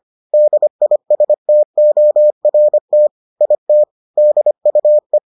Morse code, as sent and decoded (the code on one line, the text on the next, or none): -.. .. ... - --- .-. - / .. - / -.. ..- .
DISTORT IT DUE